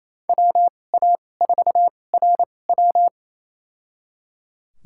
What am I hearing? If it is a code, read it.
Morse code, heard as a keyed tone, about 28 wpm, spaced as for 20 wpm, 700 Hz. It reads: WA4RW
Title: USA Call Signs - 5 Characters - 2x Character Spacing - 28wpm